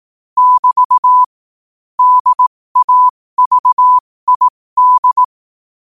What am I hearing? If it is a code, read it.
Morse code, heard as a keyed tone, about 18 wpm, spaced as for 15 wpm, 1000 Hz. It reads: <BT> DAVID